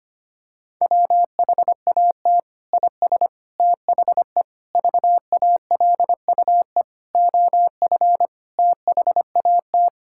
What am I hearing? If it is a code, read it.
Morse code, heard as a keyed tone, about 25 wpm, 700 Hz. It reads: WHAT IS THE VALUE OF THAT
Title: Sentences from Top 1000 Words - Review Part 6 - 25wpm